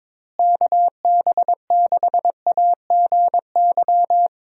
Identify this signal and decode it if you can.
Morse code, heard as a keyed tone, about 21 wpm, 700 Hz. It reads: KB6AGY
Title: USA Call Signs - 6 Characters - Repeated 3x - 22wpm